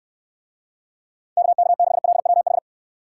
Morse code, heard as a keyed tone, about 66 wpm, 700 Hz. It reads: KK6LWH